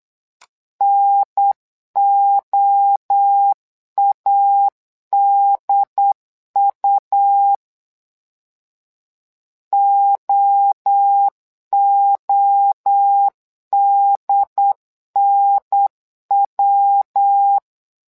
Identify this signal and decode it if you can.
Morse code, heard as a keyed tone, about 8 wpm, 800 Hz. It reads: NOADU OODNW
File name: LEZIONI / 3 MWRO / 3 MWRO 8wpm 2606